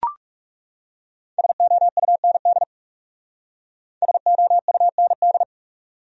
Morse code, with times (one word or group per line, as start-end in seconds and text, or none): 1.4-2.6 SOUND
4.0-5.4 SOUND